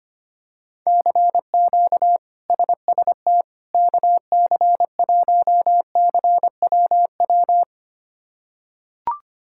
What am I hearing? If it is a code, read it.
Morse code, heard as a keyed tone, about 25 wpm, 700 Hz. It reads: CQ SST KC1CWW
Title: K1USN SST QSO Practice - 25wpm